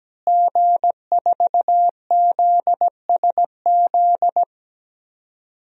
Morse code, heard as a keyed tone, about 17 wpm, 700 Hz. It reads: G4ZSZ
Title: DXCC-100 Call Signs - 5 Characters - Repeated 3x - 17wpm